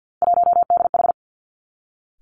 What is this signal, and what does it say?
Morse code, heard as a keyed tone, about 50 wpm, 700 Hz. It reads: 1DH
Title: Three Challenging Letter-Number Combinations 50wpm